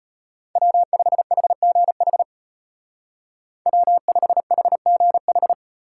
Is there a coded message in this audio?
Yes: Morse code from 0.6 to 5.5 s, heard as a keyed tone, about 32 wpm, 700 Hz. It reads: W5HGH W5HGH